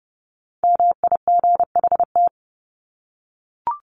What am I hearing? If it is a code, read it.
Morse code, heard as a keyed tone, about 30 wpm, 700 Hz. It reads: MIGHT
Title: Top 300 Words 30wpm